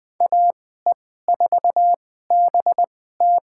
Morse code, heard as a keyed tone, about 20 wpm, spaced as for 14 wpm, 700 Hz. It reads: AE4BT
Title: USA Call Signs - 5 Characters - 2x Character Spacing - 20wpm